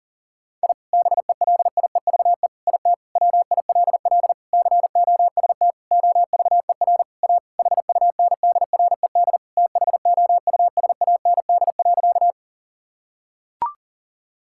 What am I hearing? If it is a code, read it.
Morse code, heard as a keyed tone, about 40 wpm, 700 Hz. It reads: I BELIEVE IT WILL COST OVER A HUNDRED THOUSAND.
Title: Sentences from Top 1000 Words - Review Part 5 - 40wpm